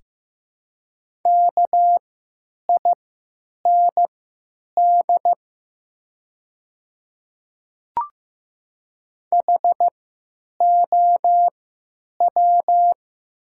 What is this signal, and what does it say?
Morse code, heard as a keyed tone, about 15 wpm, spaced as for 9 wpm, 700 Hz. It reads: KIND HOW
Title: 3x Character Spacing - Top 200 Words - 15wpm